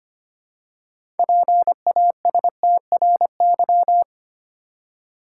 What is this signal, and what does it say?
Morse code, heard as a keyed tone, about 25 wpm, 690 Hz. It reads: PASTRY
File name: practice / 25wpm / Easy Words with Hints - 25wpm